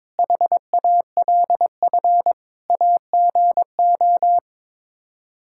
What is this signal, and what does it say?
Morse code, heard as a keyed tone, about 21 wpm, 700 Hz. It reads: HALF AGO
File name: practice / 22wpm / Sets of 6 Words - 22wpm